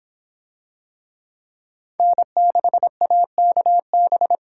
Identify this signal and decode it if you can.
Morse code, heard as a keyed tone, about 25 wpm, 700 Hz. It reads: N6AKB